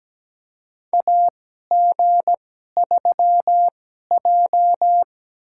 Morse code, heard as a keyed tone, about 17 wpm, spaced as for 12 wpm, 700 Hz. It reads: AG3J
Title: USA Call Signs - 4 Characters - 2x Character Spacing - 17wpm